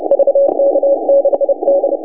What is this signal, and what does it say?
Morse code, heard as a keyed tone, about 30 wpm, 560 Hz, with noise. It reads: VK6D
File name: vk6dxi 160m s50u